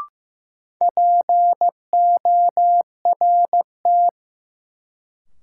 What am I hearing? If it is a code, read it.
Morse code, heard as a keyed tone, about 15 wpm, 700 Hz. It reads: PORT
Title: Sentences from Top 1000 Words - Review Part 1 - New Word Drill - 15wpm